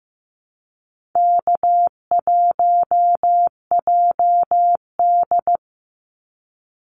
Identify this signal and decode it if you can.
Morse code, heard as a keyed tone, about 15 wpm, 700 Hz. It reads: K1JD
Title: NA SOTA Activator Callsigns 15wpm